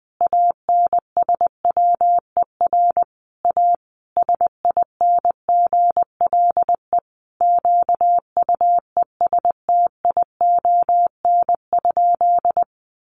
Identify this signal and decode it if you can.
Morse code, heard as a keyed tone, about 20 wpm, 700 Hz. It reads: ANSWER A SINGLE QUESTION?
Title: Sentences from Top 700 Words 20wpm